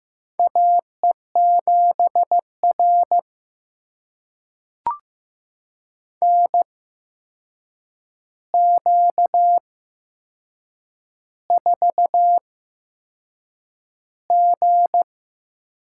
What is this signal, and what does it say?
Morse code, heard as a keyed tone, about 16 wpm, spaced as for 6 wpm, 700 Hz. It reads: AE7R NQ4G
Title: USA Call Signs - 4 Characters - 8x Character Spacing - 15wpm